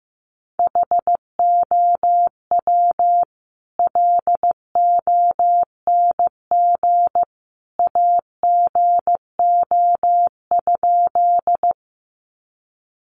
Morse code, heard as a keyed tone, about 15 wpm, 700 Hz. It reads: HOW LONG AGO?
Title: Sentences from Top 500 Words 15wpm